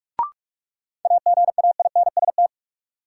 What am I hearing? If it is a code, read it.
Morse code, heard as a keyed tone, about 46 wpm, 700 Hz. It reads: AGAINST
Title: Sets of 2 Words - New Word Drill - 45wpm